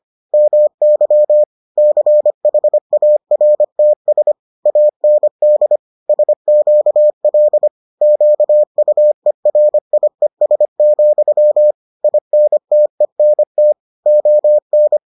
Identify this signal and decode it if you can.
Morse code, heard as a keyed tone, about 25 wpm, 600 Hz. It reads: MY CHARTS AND SQL QUERIES, INTENT ON